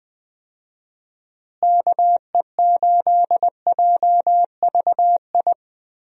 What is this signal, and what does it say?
Morse code, heard as a keyed tone, about 20 wpm, 700 Hz. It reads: KE8JVI
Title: USA Call Signs - 6 Characters - Repeated 2x - 20wpm